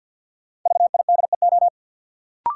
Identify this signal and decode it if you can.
Morse code, heard as a keyed tone, about 51 wpm, 700 Hz. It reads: VIDEO
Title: Sentences from Top 1000 Words - Review Part 4 - New Word Drill - 50wpm